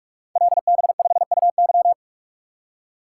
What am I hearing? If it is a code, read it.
Morse code, heard as a keyed tone, about 44 wpm, 700 Hz. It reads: LB5UY
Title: Code Groups - 5 Letter:Numbers - 45wpm